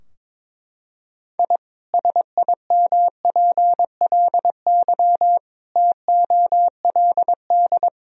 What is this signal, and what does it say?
Morse code, heard as a keyed tone, about 21 wpm, 700 Hz. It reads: I SIMPLY TOLD
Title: Sentences from Top 1000 Words - Review Part 4 - 22wpm